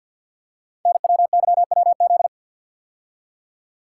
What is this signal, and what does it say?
Morse code, heard as a keyed tone, about 48 wpm, 700 Hz. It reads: NWYWZ
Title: Code Groups - 5 Letters - 50wpm